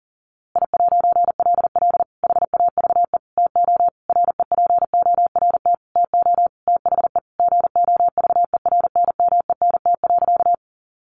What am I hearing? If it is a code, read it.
Morse code, heard as a keyed tone, about 40 wpm, 700 Hz. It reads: I'LL HAVE TO REPORT TO THE GOVERNMENT.
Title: A Sound Of Thunder 40wpm